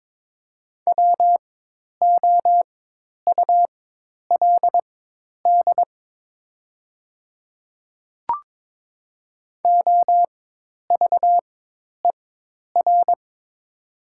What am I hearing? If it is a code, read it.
Morse code, heard as a keyed tone, about 22 wpm, spaced as for 10 wpm, 700 Hz. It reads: WOULD OVER
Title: 4x Character Spacing - Top 100 Words - 22wpm